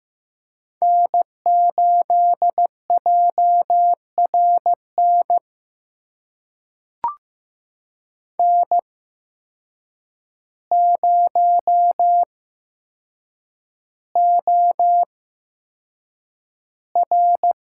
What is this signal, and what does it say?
Morse code, heard as a keyed tone, about 15 wpm, spaced as for 7 wpm, 700 Hz. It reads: N8JRN N0OR
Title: USA Call Signs - 8x Character Spacing - 15wpm